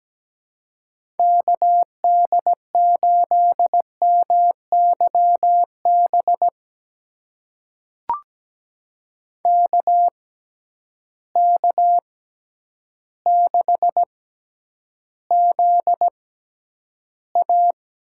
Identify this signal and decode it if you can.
Morse code, heard as a keyed tone, about 17 wpm, spaced as for 9 wpm, 700 Hz. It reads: KD8MYB KK6ZA
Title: USA Call Signs - 6x Character Spacing - 17wpm